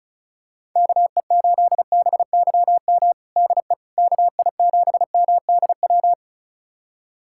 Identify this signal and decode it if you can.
Morse code, heard as a keyed tone, about 36 wpm, 700 Hz. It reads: KE8BYM DE KI7MDW
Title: USA Callsign DE Callsign - 6 Characters - 35wpm